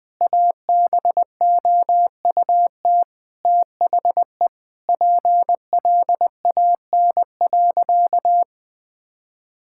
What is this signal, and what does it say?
Morse code, heard as a keyed tone, about 20 wpm, 700 Hz. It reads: ABOUT THE PLAN.
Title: Sentences with 4-Word Phrases - 20wpm